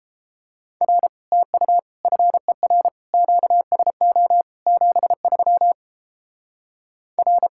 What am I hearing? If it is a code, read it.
Morse code, heard as a keyed tone, about 29 wpm, 700 Hz. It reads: R TU FER QSO 73 R